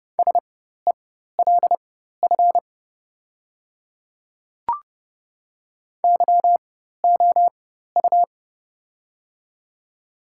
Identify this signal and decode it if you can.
Morse code, heard as a keyed tone, about 30 wpm, spaced as for 14 wpm, 700 Hz. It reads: SELF YOU
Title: 4x Character Spacing - Top 300 Words - 30wpm